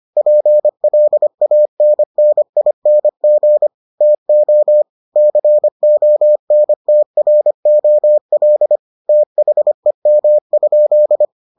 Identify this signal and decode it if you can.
Morse code, heard as a keyed tone, about 25 wpm, 600 Hz. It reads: PLANNING TO CONTROL THEM?